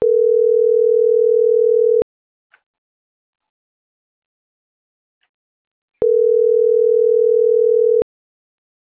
Ringback tone (North America), 440 + 480 Hz.